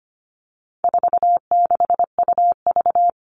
Morse code, heard as a keyed tone, about 25 wpm, 700 Hz. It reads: 46UV
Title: All Cut Numbers - 4-Digit 25wpm